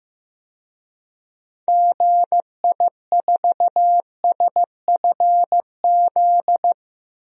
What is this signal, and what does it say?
Morse code, heard as a keyed tone, about 15 wpm, 700 Hz. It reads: GI4SFZ
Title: DXCC-100 Call Signs - 6 Characters - Repeated 2x - 15wpm